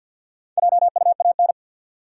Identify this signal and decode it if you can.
Morse code, heard as a keyed tone, about 48 wpm, 700 Hz. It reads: JUAN
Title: USA Common Names - Max 5 Letters - 50wpm